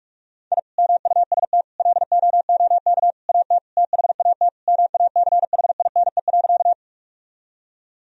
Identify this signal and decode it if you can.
Morse code, heard as a keyed tone, about 45 wpm, 700 Hz. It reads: I MUST LOOK AT THAT MACHINE.
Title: Sentences from Top 1000 Words - Review Part 2 - 45wpm